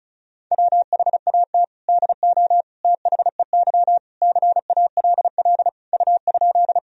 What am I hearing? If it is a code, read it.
Morse code, heard as a keyed tone, about 36 wpm, 700 Hz. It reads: WHAT DO THEY CALL U?
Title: Sentences from Top 100 Words and Abbreviations 35wpm